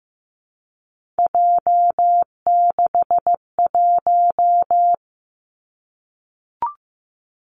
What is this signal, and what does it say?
Morse code, heard as a keyed tone, about 15 wpm, 700 Hz. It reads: J61
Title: Three Challenging Letter-Number Combinations 15wpm